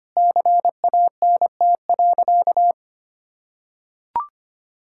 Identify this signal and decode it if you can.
Morse code, heard as a keyed tone, about 25 wpm, 700 Hz. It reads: CANT.
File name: practice / 25wpm / Sentences with 2-Word Phrases - 25wpm